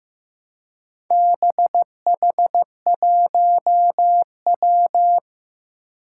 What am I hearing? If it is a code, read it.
Morse code, heard as a keyed tone, about 15 wpm, 700 Hz. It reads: BH1W
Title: Four Challenging Letter-Number Combinations 15wpm